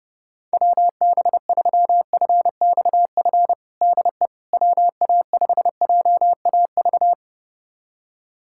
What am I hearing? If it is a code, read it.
Morse code, heard as a keyed tone, about 32 wpm, 700 Hz. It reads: WB3FXF DE WA5JAV
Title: USA Callsign DE Callsign 30wpm